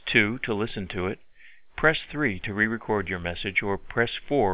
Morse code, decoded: none